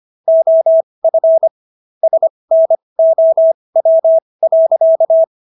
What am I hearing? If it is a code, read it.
Morse code, heard as a keyed tone, about 25 wpm, spaced as for 20 wpm, 650 Hz. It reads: OF SNOW.